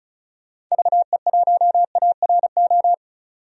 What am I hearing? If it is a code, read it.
Morse code, heard as a keyed tone, about 36 wpm, 700 Hz. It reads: VE1ARO
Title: DXCC-100 Call Signs - 6 Characters - 35wpm